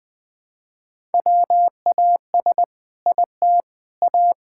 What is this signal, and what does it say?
Morse code, heard as a keyed tone, about 20 wpm, 700 Hz. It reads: WAS IT A
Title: Sentences from Top 1000 Words - Review Part 6 - 20wpm